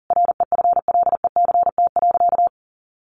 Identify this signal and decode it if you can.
Morse code, heard as a keyed tone, about 40 wpm, 700 Hz. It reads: REFLECT.
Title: Sentences from Top 1500 Words 40wpm